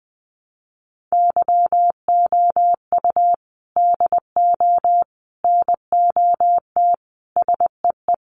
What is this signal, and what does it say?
Morse code, heard as a keyed tone, about 20 wpm, 700 Hz. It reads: YOU DO NOT SEE
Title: Sentences from Top 1300 Words 20wpm